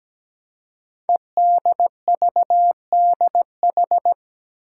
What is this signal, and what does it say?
Morse code, heard as a keyed tone, about 17 wpm, 700 Hz. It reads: EDVDH
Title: Code Groups - 5 Letters - 17wpm